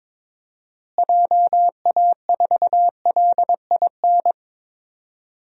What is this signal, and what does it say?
Morse code, heard as a keyed tone, about 22 wpm, 700 Hz. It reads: JA4LIN
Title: DXCC-100 Call Signs - 6 Characters - Repeated 2x - 22wpm